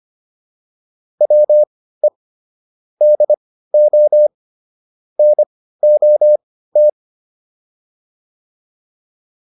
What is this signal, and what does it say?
Morse code, heard as a keyed tone, about 25 wpm, spaced as for 15 wpm, 600 Hz. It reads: WE DO NOT